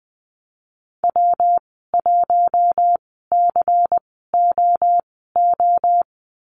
Morse code, heard as a keyed tone, about 20 wpm, spaced as for 15 wpm, 700 Hz. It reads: W1COO
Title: 167 Lesson 27g - Number 7 - Call Signs 20wpm